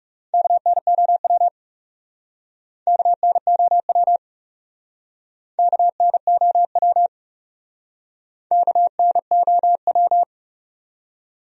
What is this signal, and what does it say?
Morse code, heard as a keyed tone, about 40 wpm, 700 Hz. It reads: KNOW KNOW KNOW KNOW